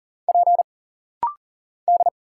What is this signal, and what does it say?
Morse code, heard as a keyed tone, about 40 wpm, 700 Hz. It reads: P D
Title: Single Letters - Rapid-Fire 40wpm